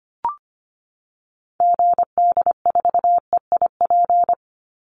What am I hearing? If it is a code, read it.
Morse code, heard as a keyed tone, about 25 wpm, 700 Hz. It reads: GD4EIP
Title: Call Signs Repeated 3x - 25wpm